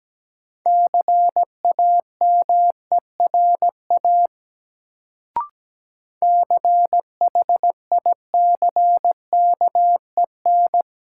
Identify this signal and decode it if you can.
Morse code, heard as a keyed tone, about 17 wpm, 700 Hz. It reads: CAMERA CHICKEN
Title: Sentences from Top 1000 Words - Review Part 5 - New Word Drill - 17wpm